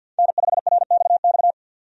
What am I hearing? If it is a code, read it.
Morse code, heard as a keyed tone, about 45 wpm, 700 Hz. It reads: N5RXX
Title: USA Call Signs - 5 Characters - 50wpm